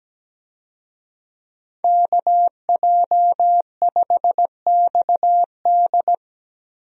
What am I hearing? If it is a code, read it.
Morse code, heard as a keyed tone, about 17 wpm, 700 Hz. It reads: KJ5XD
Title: USA Call Signs - 5 Characters - Repeated 2x - 17wpm